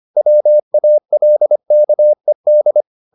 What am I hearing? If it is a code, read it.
Morse code, heard as a keyed tone, about 25 wpm, 600 Hz. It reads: WALKED
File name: the old man and the sea part 1 25wpm